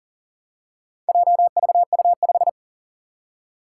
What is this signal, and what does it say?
Morse code, heard as a keyed tone, about 40 wpm, 700 Hz. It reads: JVU5
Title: Four Challenging Letter-Number Combinations 40wpm